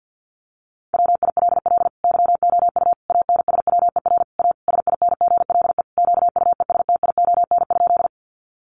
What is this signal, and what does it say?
Morse code, heard as a keyed tone, about 50 wpm, 700 Hz. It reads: WILL YOU ANSWER A SINGLE QUESTION?
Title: Sentences from Top 700 Words 50wpm